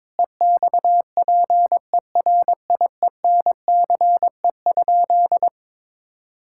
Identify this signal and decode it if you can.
Morse code, heard as a keyed tone, about 22 wpm, 700 Hz. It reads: EXPERIENCE?
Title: Sentences from Top 1600 Words 22wpm